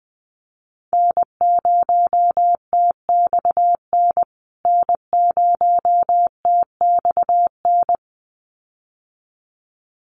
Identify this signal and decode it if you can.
Morse code, heard as a keyed tone, about 20 wpm, 700 Hz. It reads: N0TXN N0TXN
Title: USA Call Signs Repeated 2x 20wpm